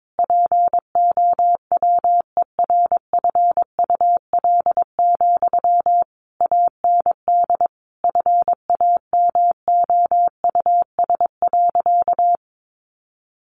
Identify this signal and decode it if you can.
Morse code, heard as a keyed tone, about 22 wpm, 700 Hz. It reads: POWERFUL, AND FAMOUS.
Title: Sentences from Top 1500 Words 22wpm